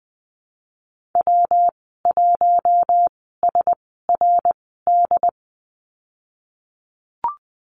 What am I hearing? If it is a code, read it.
Morse code, heard as a keyed tone, about 20 wpm, spaced as for 15 wpm, 700 Hz. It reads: W1SRD
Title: 128 Lesson 21g - Number 6 - Call Signs 20wpm